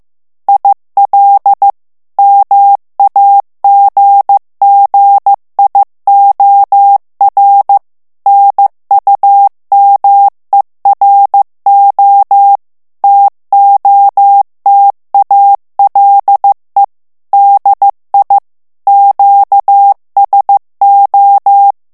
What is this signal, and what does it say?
Morse code, heard as a keyed tone, about 15 wpm, 800 Hz. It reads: IL MAGGIOR NUMERO TOTALE DI QSO
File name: morse mp3 / ITALIEN texte clair 2 vitesse 14 mots minute